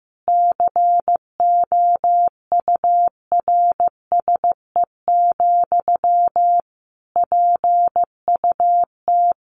Morse code, heard as a keyed tone, about 14 wpm, 700 Hz. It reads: COURSE, PUT
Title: Sentences from Top 1400 Words 15wpm